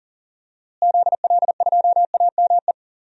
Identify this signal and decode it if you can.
Morse code, heard as a keyed tone, about 40 wpm, 700 Hz. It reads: ZL2AME
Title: DXCC-100 Call Signs - 6 Characters - Repeated 2x - 40wpm